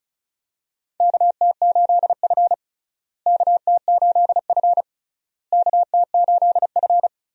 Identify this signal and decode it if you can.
Morse code, heard as a keyed tone, about 36 wpm, 700 Hz. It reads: KT8F KT8F KT8F
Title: USA Call Signs - 4 Characters - Repeated 3x - 35wpm